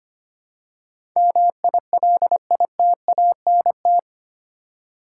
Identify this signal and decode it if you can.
Morse code, heard as a keyed tone, about 25 wpm, 700 Hz. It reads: MILITANT
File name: 8 Letter Words - First 3 Letters Given - 25wpm